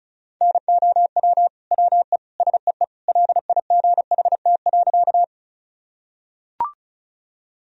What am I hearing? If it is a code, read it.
Morse code, heard as a keyed tone, about 36 wpm, 700 Hz. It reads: NOW WE SEE LIGHT.
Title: Sentences from Top 1000 Words - Review Part 1 - 35wpm